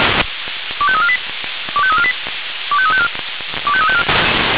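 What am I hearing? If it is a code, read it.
Five-tone selective call (ZVEI), heard as quick runs of five tones, tones of about 70 ms. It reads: 25348 25348 25354 25354
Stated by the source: ZVEI five-tone selective call received on SDR.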